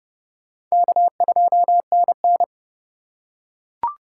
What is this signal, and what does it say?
Morse code, heard as a keyed tone, about 29 wpm, 700 Hz. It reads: K2NN
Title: USA Call Signs - 3x Character Spacing - 30wpm